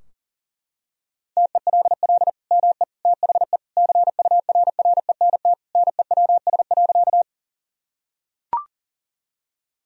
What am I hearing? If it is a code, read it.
Morse code, heard as a keyed tone, about 37 wpm, 700 Hz. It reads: TELL ME THE CURRENT NEWS.
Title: Sentences from Top 1000 Words - Review Part 4 - 40wpm